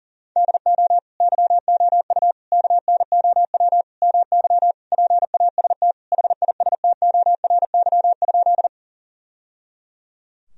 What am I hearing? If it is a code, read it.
Morse code, heard as a keyed tone, about 40 wpm, 700 Hz. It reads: DO YOU KNOW MY PAST HISTORY?
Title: Sentences from Top 1000 Words - Review Part 3 - 40wpm